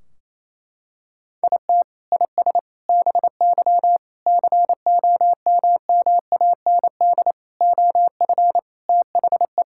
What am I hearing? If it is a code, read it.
Morse code, heard as a keyed tone, about 27 wpm, 700 Hz. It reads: IT IS BY COMMAND OF THE
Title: Sentences from Top 1600 Words 28wpm